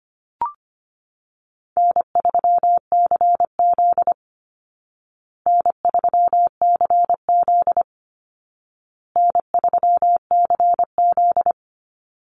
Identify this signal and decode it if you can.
Morse code, heard as a keyed tone, about 25 wpm, 700 Hz. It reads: N3CZ N3CZ N3CZ
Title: Call Signs Repeated 2x - 25wpm